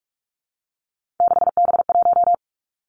Morse code, heard as a keyed tone, about 46 wpm, 700 Hz. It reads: 6B1
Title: Three Challenging Letter-Number Combinations 45wpm